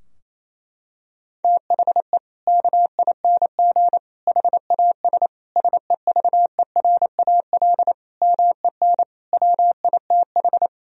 Morse code, heard as a keyed tone, about 28 wpm, 700 Hz. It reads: THE KING HAS SEVERAL MEN WITH
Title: Sentences from Top 1000 Words - Review Part 2 - 28wpm